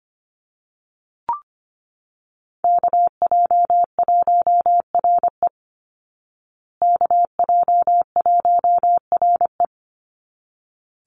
Morse code, heard as a keyed tone, about 25 wpm, 700 Hz. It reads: KJ1RE KJ1RE
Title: Call Signs Repeated 3x - 25wpm